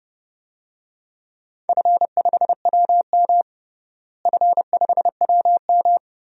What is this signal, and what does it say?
Morse code, heard as a keyed tone, about 30 wpm, 700 Hz. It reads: F5WM F5WM
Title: DXCC-100 Call Signs - 4 Characters - Repeated 3x - 30wpm